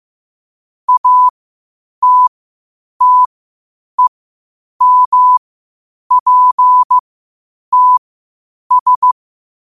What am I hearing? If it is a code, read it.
Morse code, heard as a keyed tone, about 15 wpm, spaced as for 8 wpm, 1000 Hz. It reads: ATTEMPTS
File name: qotdfiles / 8wpm